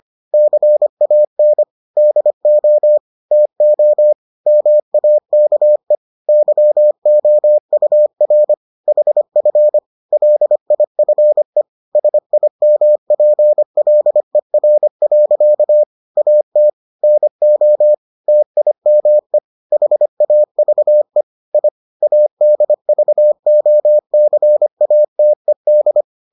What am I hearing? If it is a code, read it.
Morse code, heard as a keyed tone, about 25 wpm, 600 Hz. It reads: CAN DO TO MAKE YOUR HF LIFE SIMPLER. AT NO TIME HAVE I ADVOCATED